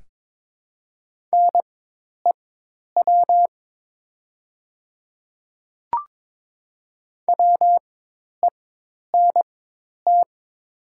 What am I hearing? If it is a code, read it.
Morse code, heard as a keyed tone, about 22 wpm, spaced as for 10 wpm, 700 Hz. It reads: NEW WENT
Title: 4x Character Spacing - Top 200 Words - 22wpm